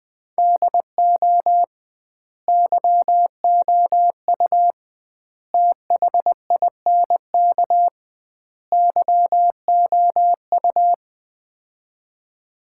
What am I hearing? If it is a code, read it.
Morse code, heard as a keyed tone, about 20 wpm, 700 Hz. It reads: DO YOU THINK YOU
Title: 2x Word-Spacing - Sets of 4 Words - 20wpm